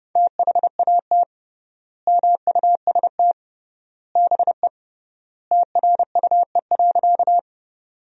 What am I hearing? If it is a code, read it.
Morse code, heard as a keyed tone, about 30 wpm, 700 Hz. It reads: THAT MUST BE TRUE.
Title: 3x Word-Spacing - Sentences from Top 400 Words - 30wpm